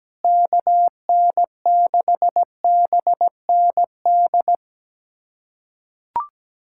Morse code, heard as a keyed tone, about 17 wpm, 700 Hz. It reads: KN6BND
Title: USA Callsign DE Callsign 17wpm